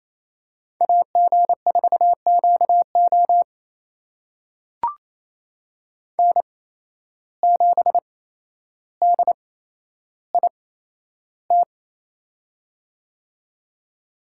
Morse code, heard as a keyed tone, about 28 wpm, spaced as for 12 wpm, 700 Hz. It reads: AG4QO N7DIT
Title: USA Call Signs - 5 Characters - 8x Character Spacing - 28wpm